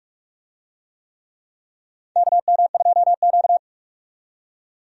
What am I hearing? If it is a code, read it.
Morse code, heard as a keyed tone, about 46 wpm, 700 Hz. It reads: KM2Q
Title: USA Call Signs - 4 Characters - 45wpm